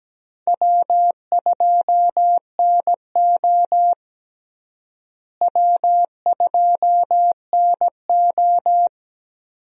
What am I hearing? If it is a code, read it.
Morse code, heard as a keyed tone, about 17 wpm, 700 Hz. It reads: W2NO W2NO
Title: USA Call Signs - 4 Characters - Repeated 2x - 17wpm